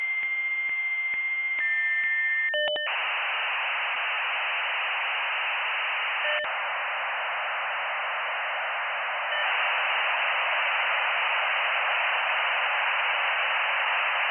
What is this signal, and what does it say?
Dial-up modem handshake.
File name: old-dial-up-sound-1984